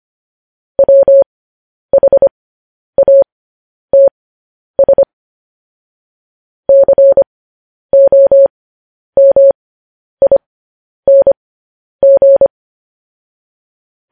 Morse code, heard as a keyed tone, about 25 wpm, spaced as for 10 wpm, 550 Hz. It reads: WHATS COMING
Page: news headlines in morse code at 10 wpm